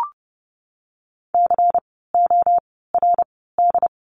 Morse code, heard as a keyed tone, about 30 wpm, spaced as for 17 wpm, 700 Hz. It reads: CORD
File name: Lesson 16f - Letter c - Words 30wpm